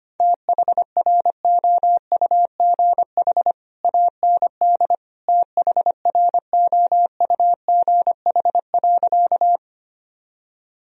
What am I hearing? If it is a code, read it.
Morse code, heard as a keyed tone, about 25 wpm, 700 Hz. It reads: THROUGH AND THROUGH.
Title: Sentences from Top 1000 Words - Review Part 1 - 25wpm